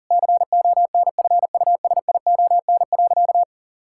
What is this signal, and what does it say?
Morse code, heard as a keyed tone, about 40 wpm, 700 Hz. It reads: CONFUSION.
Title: Sentences from Top 1600 Words 40wpm